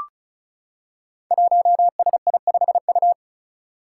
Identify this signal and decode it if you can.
Morse code, heard as a keyed tone, about 36 wpm, 700 Hz. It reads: 1SI5U
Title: Five Challenging Letter-Number Combinations 35wpm